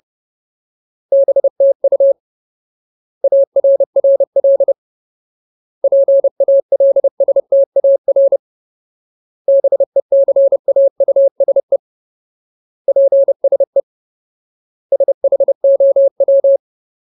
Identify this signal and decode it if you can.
Morse code, heard as a keyed tone, about 30 wpm, 550 Hz. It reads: BTU ARRL PALSTAR BECAUSE PSE SHOW
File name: Ultimate Word List / 30WPM Ultimate